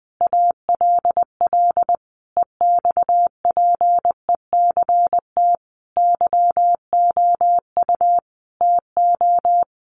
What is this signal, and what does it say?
Morse code, heard as a keyed tone, about 20 wpm, 700 Hz. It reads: ALL EXPECT YOU TO
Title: Sentences from Top 1100 Words 20wpm